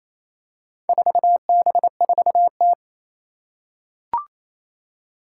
Morse code, heard as a keyed tone, about 27 wpm, 700 Hz. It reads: 4B4T
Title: Four Challenging Letter-Number Combinations 28wpm